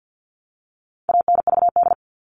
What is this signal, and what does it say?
Morse code, heard as a keyed tone, about 50 wpm, 700 Hz. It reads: ANVD
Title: All Cut Numbers - 4-Digit 50wpm